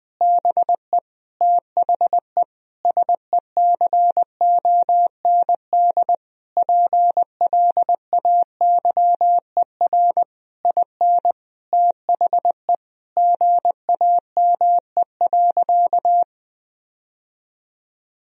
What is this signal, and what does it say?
Morse code, heard as a keyed tone, about 21 wpm, 700 Hz. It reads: BE THE SECOND PLAYER IN THE GAME.
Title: Sentences from Top 1000 Words - Review Part 4 - 20wpm